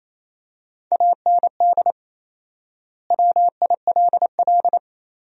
Morse code, heard as a keyed tone, about 28 wpm, 700 Hz. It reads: AND WILL
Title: 4x Word-Spacing - Sentences from Top 400 Words - 28wpm